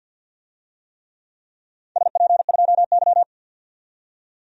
Morse code, heard as a keyed tone, about 51 wpm, 700 Hz. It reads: SP2Y
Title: DXCC-100 Call Signs - 4 Characters - 50wpm